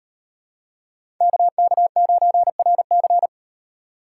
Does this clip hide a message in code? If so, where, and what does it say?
Yes: Morse code from 1.2 to 3.3 s, heard as a keyed tone, about 36 wpm, 700 Hz. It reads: KK9RC